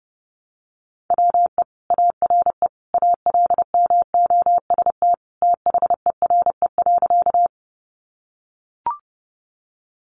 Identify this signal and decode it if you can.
Morse code, heard as a keyed tone, about 30 wpm, 700 Hz. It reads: WE ARE ALMOST THERE.
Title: Sentences from Top 700 Words 30wpm